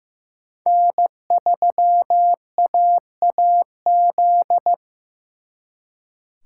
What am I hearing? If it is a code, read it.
Morse code, heard as a keyed tone, about 18 wpm, 700 Hz. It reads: N3AAZ
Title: Speed Racing - USA 1x3 Call Signs 15wpm